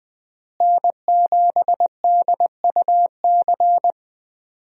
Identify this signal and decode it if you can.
Morse code, heard as a keyed tone, about 20 wpm, 700 Hz. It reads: N7DUC
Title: USA Call Signs - 5 Characters - Repeated 3x - 20wpm